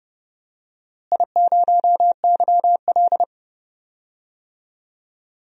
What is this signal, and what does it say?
Morse code, heard as a keyed tone, about 30 wpm, 700 Hz. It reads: I0YL
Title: DXCC-100 Call Signs - 4 Characters - Repeated 3x - 30wpm